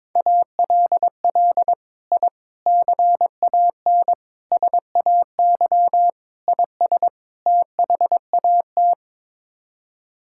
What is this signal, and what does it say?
Morse code, heard as a keyed tone, about 22 wpm, 700 Hz. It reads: ALL I CAN SAY IS THAT
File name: practice / 22wpm / Sets of 6 Words - 22wpm